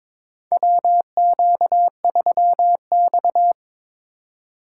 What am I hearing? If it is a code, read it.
Morse code, heard as a keyed tone, about 22 wpm, 700 Hz. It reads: WQ3X
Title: USA Call Signs - 4 Characters - 4x Character Spacing - 22wpm